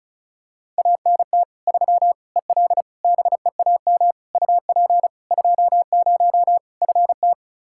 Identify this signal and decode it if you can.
Morse code, heard as a keyed tone, about 36 wpm, 700 Hz. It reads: ANT 3 EL BEAM UP 20 FT
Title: Practice QSO - Exchange 2 - Info - 35wpm